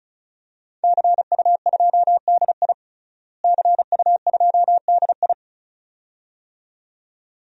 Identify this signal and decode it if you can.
Morse code, heard as a keyed tone, about 34 wpm, 700 Hz. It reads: CU2DI CU2DI
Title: DXCC-100 Call Signs - 5 Characters - Repeated 2x - 35wpm